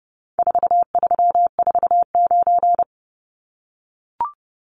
Morse code, heard as a keyed tone, about 30 wpm, 700 Hz. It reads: 4349
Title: Numbers - 4-Digit 30wpm